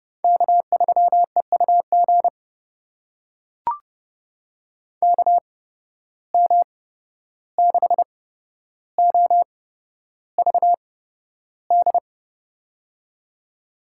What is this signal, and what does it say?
Morse code, heard as a keyed tone, about 32 wpm, spaced as for 12 wpm, 700 Hz. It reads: K3EUG KM6OVD
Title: USA Call Signs - 8x Character Spacing - 30wpm